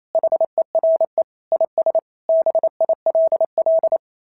Morse code, heard as a keyed tone, about 28 wpm, 690 Hz. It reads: HERE IS BILL
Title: SOTA QSO Practice 28wpm